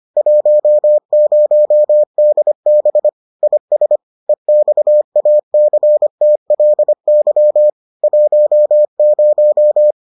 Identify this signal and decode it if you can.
Morse code, heard as a keyed tone, about 25 wpm, 600 Hz. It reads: 10DB IS EXACTLY 10